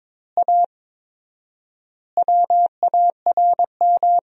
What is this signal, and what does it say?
Morse code, heard as a keyed tone, about 22 wpm, 700 Hz. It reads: A WARM
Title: 4x Word-Spacing - Sentences from Top 500 Words - 22wpm